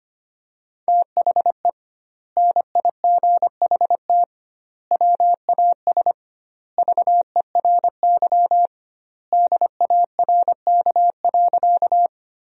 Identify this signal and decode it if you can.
Morse code, heard as a keyed tone, about 25 wpm, 700 Hz. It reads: THE NIGHT WAS VERY DARK.
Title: 2x Word-Spacing - Sentences from Top 500 Words - 25wpm